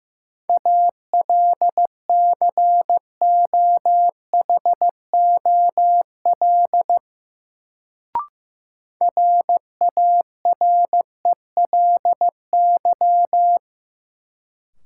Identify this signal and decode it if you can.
Morse code, heard as a keyed tone, about 15 wpm, 700 Hz. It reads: ALCOHOL RARELY
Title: Sentences from Top 1400 Words - New Word Drill Alt - 15wpm